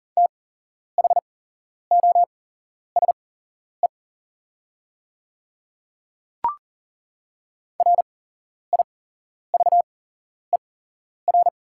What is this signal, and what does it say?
Morse code, heard as a keyed tone, about 38 wpm, spaced as for 11 wpm, 690 Hz. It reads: THOSE RIVER
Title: 8x Character Spacing - Top 500 Words - 40wpm